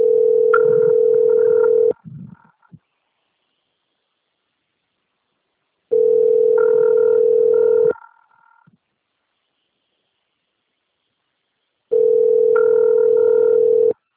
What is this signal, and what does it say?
Ringback tone (North America), 440 + 480 Hz.